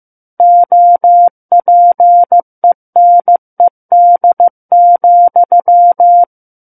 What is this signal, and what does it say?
Morse code, heard as a keyed tone, about 15 wpm, 700 Hz. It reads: OPENED,
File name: Book One / 06-Chapter 6 - 15 WPM